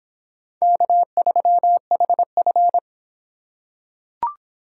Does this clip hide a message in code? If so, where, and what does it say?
Yes: Morse code from 0.6 to 2.8 s, heard as a keyed tone, about 24 wpm, 700 Hz. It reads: K3HF